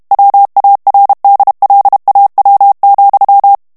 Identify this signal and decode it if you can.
Morse code, heard as a keyed tone, about 32 wpm, 800 Hz. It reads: WARDLAW,
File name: morse mp3 / ESPAGNOL TEXTE EN CLAIR 1 VITESSE 30 MOTS MINUTES